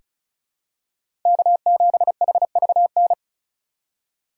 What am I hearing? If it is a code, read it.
Morse code, heard as a keyed tone, about 36 wpm, 700 Hz. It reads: K7HVN
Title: USA Call Signs - 3x Character Spacing - 35wpm